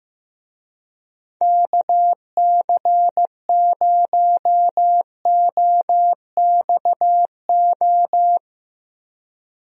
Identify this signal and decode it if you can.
Morse code, heard as a keyed tone, about 15 wpm, 700 Hz. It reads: KC0OXO
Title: USA Call Signs - 6 Characters - Repeated 2x - 15wpm